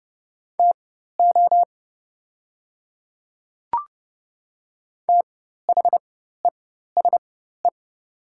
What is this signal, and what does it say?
Morse code, heard as a keyed tone, about 30 wpm, spaced as for 14 wpm, 700 Hz. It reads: TO THESE
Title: 4x Character Spacing - Top 100 Words - 30wpm